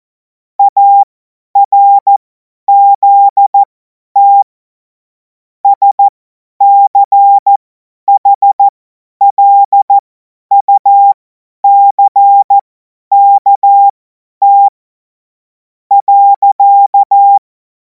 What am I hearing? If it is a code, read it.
Morse code, heard as a keyed tone, about 14 wpm, spaced as for 9 wpm, 800 Hz. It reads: ARZT SCHLUCKT.
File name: Morsekurs / Witze / Witz 14. Juli 2026 50 F70